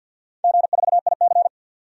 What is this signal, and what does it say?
Morse code, heard as a keyed tone, about 49 wpm, 700 Hz. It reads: G4IC
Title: DXCC-100 Call Signs - 4 Characters - 50wpm